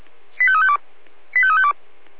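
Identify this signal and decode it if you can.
Five-tone selective call (ZVEI), heard as quick runs of five tones, tones of about 70 ms. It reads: 85352 85352
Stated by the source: ZVEI five-tone selective call 85350 received by SDR.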